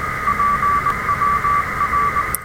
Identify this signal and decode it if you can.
Morse code, heard as a keyed tone, about 20 wpm, 1160 Hz, with radio noise. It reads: PWR
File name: attachments / R7AO QRP 27.03.2016 Home made TRX UA1CEG